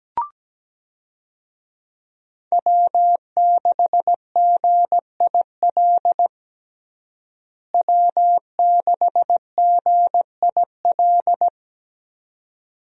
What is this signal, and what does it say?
Morse code, heard as a keyed tone, about 17 wpm, 700 Hz. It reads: W6GIL W6GIL
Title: USA Call Signs - 5 Characters - Repeated 2x - 17wpm